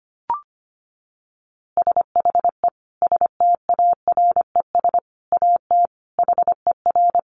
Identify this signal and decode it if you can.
Morse code, heard as a keyed tone, about 25 wpm, 700 Hz. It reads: SHE STARES AT HER
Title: Easy Phrases 25wpm